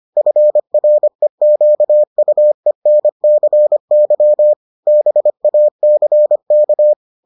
Morse code, heard as a keyed tone, about 25 wpm, 600 Hz. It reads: FREQUENCY BACK